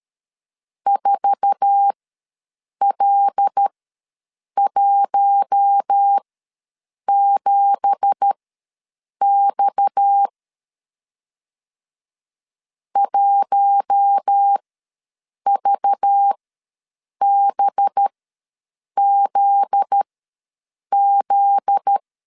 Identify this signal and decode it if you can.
Morse code, heard as a keyed tone, about 12 wpm, spaced as for 7 wpm, 800 Hz. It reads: 4L17X 1VBZZ
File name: LEZIONI / 9 2867 ws 12wpm 3431